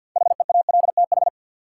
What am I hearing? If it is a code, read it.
Morse code, heard as a keyed tone, about 53 wpm, 700 Hz. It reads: HEALTH
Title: Sentences from Top 600 Words - New Word Drill - 50wpm